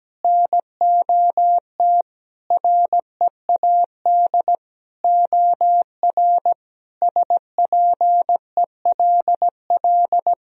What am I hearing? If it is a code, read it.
Morse code, heard as a keyed tone, about 17 wpm, 700 Hz. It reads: NOT READ OR SPELL
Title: Sentences from Top 1000 Words - Review Part 3 - 17wpm